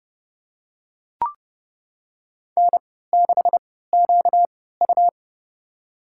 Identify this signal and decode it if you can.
Morse code, heard as a keyed tone, about 31 wpm, spaced as for 17 wpm, 700 Hz. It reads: N6QU
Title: USA Call Signs - 4 Characters - 3x Character Spacing - 30wpm